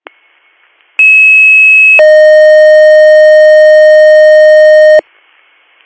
Two-tone page, heard as a one-second tone, then a three-second tone, about 2600 then 620 Hz. It is heard from 1.0 to 5.0 s.